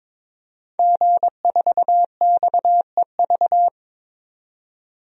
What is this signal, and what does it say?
Morse code, heard as a keyed tone, about 22 wpm, 700 Hz. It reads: G4XEV
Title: DXCC-100 Call Signs - 5 Characters - 22wpm